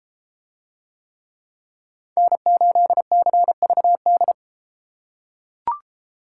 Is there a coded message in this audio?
Yes: Morse code from 2.2 to 4.3 s, heard as a keyed tone, about 31 wpm, 700 Hz. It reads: N8CVD